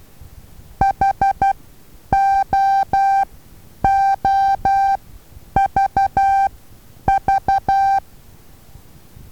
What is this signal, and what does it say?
Morse code, heard as a keyed tone, about 12 wpm, spaced as for 9 wpm, 780 Hz. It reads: HOOVV
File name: Taviro HA6VA A